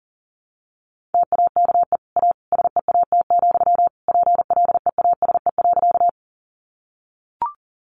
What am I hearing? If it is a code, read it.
Morse code, heard as a keyed tone, about 40 wpm, 700 Hz. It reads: TAKE A SEAT, PLEASE.
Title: Sentences from Top 1500 Words 40wpm